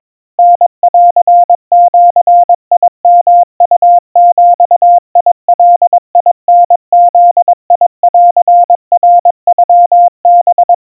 Morse code (Japanese Wabun code), heard as a keyed tone, about 22 wpm, 700 Hz. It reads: タンジヨウビガタブンナノハ
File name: 80 cpm 700 Hz Crispy